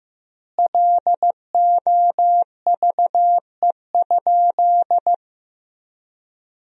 Morse code, heard as a keyed tone, about 15 wpm, 700 Hz. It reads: LOVE?
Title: 3x Word-Spacing - Sentences from Top 500 Words - 15wpm